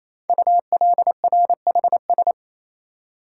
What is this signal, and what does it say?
Morse code, heard as a keyed tone, about 27 wpm, 700 Hz. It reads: ULRHS
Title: Code Groups - 5 Letters - 28wpm